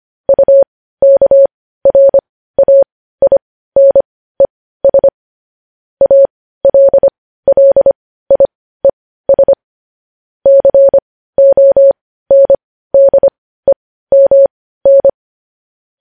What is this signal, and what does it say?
Morse code, heard as a keyed tone, about 25 wpm, spaced as for 15 wpm, 550 Hz. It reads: UKRAINES ALLIES CONDEMN
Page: News Headlines in Morse Code at 15 WPM